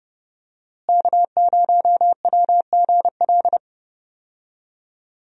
Morse code, heard as a keyed tone, about 30 wpm, 700 Hz. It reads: K0WGL
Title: USA Call Signs - 5 Characters - Repeated 3x - 30wpm